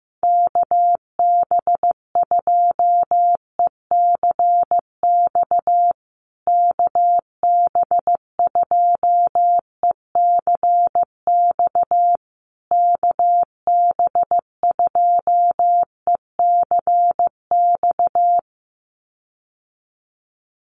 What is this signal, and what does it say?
Morse code, heard as a keyed tone, about 14 wpm, 700 Hz. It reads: KB2ECX KB2ECX KB2ECX
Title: USA Call Signs Repeated 3x 15wpm